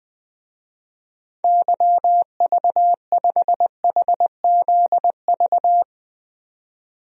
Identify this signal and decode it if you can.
Morse code, heard as a keyed tone, about 21 wpm, 700 Hz. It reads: YV5HZV
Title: DXCC-100 Call Signs - 6 Characters - 20wpm